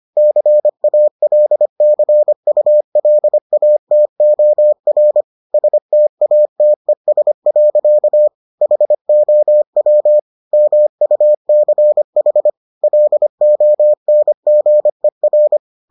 Morse code, heard as a keyed tone, about 25 wpm, 600 Hz. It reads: CALCULATOR STATES. HOW MUCH LONGER